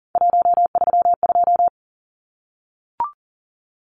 Morse code, heard as a keyed tone, about 40 wpm, 700 Hz. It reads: 132
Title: Numbers - 3-Digit 40wpm